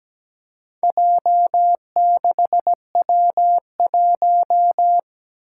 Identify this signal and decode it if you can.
Morse code, heard as a keyed tone, about 17 wpm, 700 Hz. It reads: J6W1
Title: Four Challenging Letter-Number Combinations 17wpm